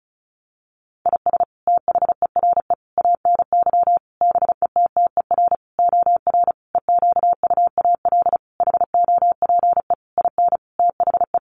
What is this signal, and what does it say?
Morse code, heard as a keyed tone, about 34 wpm, 700 Hz. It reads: IS THERE ANY BETTER OR EQUAL HOPE IN THE
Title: Sentences from Top 1400 Words 35wpm